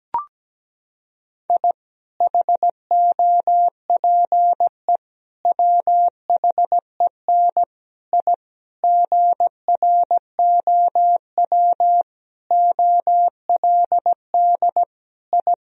Morse code, heard as a keyed tone, about 17 wpm, 700 Hz. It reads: I HOPE WHEN I GROW OLD I
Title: Sentences from Top 1000 Words - Review Part 1 - 17wpm